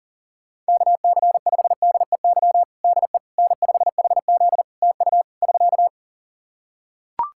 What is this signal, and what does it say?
Morse code, heard as a keyed tone, about 37 wpm, 700 Hz. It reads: KC5DEY DE N5HZ TU <SK>
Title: Practice QSO - Exchange 4 - Ending - 40wpm